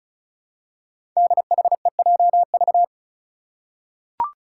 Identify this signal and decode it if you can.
Morse code, heard as a keyed tone, about 34 wpm, 700 Hz. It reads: DHEJV
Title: Five Challenging Letter-Number Combinations 35wpm